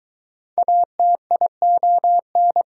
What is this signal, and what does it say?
Morse code, heard as a keyed tone, about 22 wpm, 700 Hz. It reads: ATION